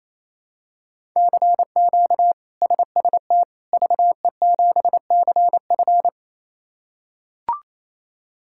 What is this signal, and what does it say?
Morse code, heard as a keyed tone, about 28 wpm, 700 Hz. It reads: CQ SST VE7CF
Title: K1USN SST QSO Practice - 28wpm